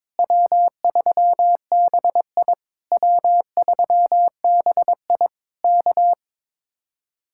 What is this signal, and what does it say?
Morse code, heard as a keyed tone, about 22 wpm, 700 Hz. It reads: W3BI W3BI K
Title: Speed-Racing Common QSO Phrases - 22wpm